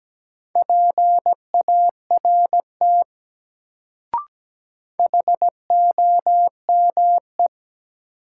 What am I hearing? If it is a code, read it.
Morse code, heard as a keyed tone, about 17 wpm, 700 Hz. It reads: PART HOME
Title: Sentences from Top 200 Words - New Word Drill - 17wpm